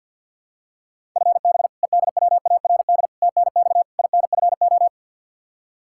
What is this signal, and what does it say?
Morse code, heard as a keyed tone, about 52 wpm, 700 Hz. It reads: FB EDWARD TNX INFO
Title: Practice QSO - Exchange 3 - More Info - 50wpm